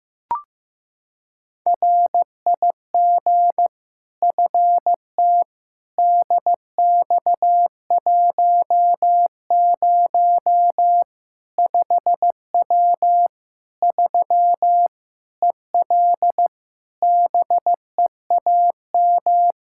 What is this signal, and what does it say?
Morse code, heard as a keyed tone, about 15 wpm, 700 Hz. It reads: RIG FT DX10 5W 3 EL BEAM
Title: Practice QSO - Exchange 2 - Info - 15wpm